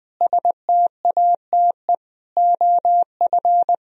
Morse code, heard as a keyed tone, about 21 wpm, 700 Hz. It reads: STATE OF
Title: Sentences from Top 1000 Words - Review Part 4 - 20wpm